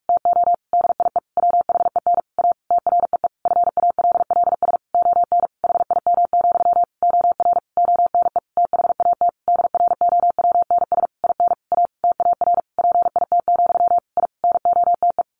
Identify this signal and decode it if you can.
Morse code, heard as a keyed tone, about 44 wpm, 700 Hz. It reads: TO DIE WHEN A TREE FALLS ON HIM, OR ONE THAT DROWNS IN A TAR PIT, I NOTE
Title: A Sound Of Thunder 45wpm